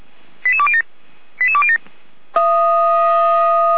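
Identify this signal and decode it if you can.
Five-tone selective call (ZVEI), heard as quick runs of five tones, tones of about 70 ms. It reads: 88207 88207
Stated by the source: ZVEI five-tone selective call received on an SDR.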